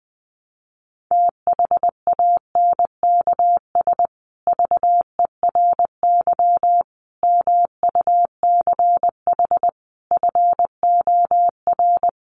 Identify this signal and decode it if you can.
Morse code, heard as a keyed tone, about 20 wpm, 700 Hz. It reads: THANKS VERY MUCH FOR
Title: Sets of 5 Words Encore2 20wpm